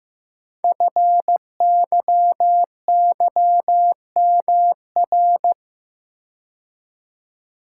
Morse code, heard as a keyed tone, about 15 wpm, 700 Hz. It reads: FYYMR
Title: Code Groups - 5 Letter:Numbers - 15wpm